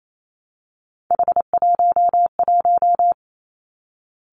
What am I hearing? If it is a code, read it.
Morse code, heard as a keyed tone, about 28 wpm, 700 Hz. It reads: H11
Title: Three Challenging Letter-Number Combinations 28wpm